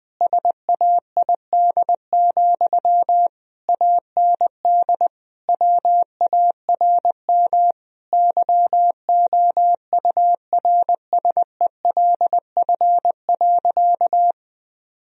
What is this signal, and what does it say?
Morse code, heard as a keyed tone, about 21 wpm, 700 Hz. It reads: SAID, AND WARM YOURSELF.
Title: Sentences from Top 1000 Words - Review Part 2 - 20wpm